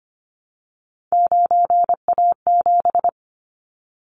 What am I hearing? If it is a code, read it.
Morse code, heard as a keyed tone, about 25 wpm, 700 Hz. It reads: 9A7
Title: Callsign Prefixes for Top 25 DXCC - 25wpm